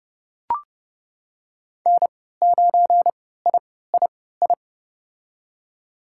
Morse code, heard as a keyed tone, about 30 wpm, spaced as for 17 wpm, 700 Hz. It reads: N9III
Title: USA Call Signs - 3x Character Spacing - 30wpm